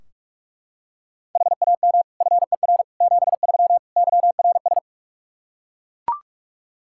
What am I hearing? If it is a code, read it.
Morse code, heard as a keyed tone, about 41 wpm, 700 Hz. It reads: HAM FER 73 YRS
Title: Practice QSO - Exchange 3 - More Info - 45wpm